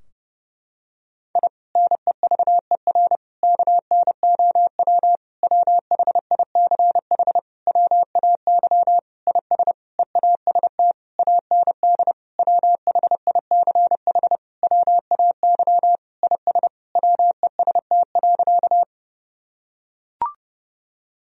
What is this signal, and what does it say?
Morse code, heard as a keyed tone, about 30 wpm, 700 Hz. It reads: I NEVER KNOW WHICH WAY IS EAST AND WHICH WAY IS WEST.
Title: Sentences from Top 1000 Words - Review Part 3 - 30wpm